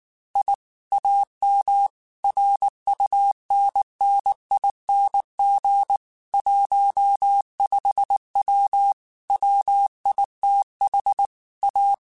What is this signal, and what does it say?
Morse code, heard as a keyed tone, about 19 wpm, 800 Hz. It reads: I AM RUNNING 15W WITH A
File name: QSOs / QSOs-mp3s / WD0BGP 1@18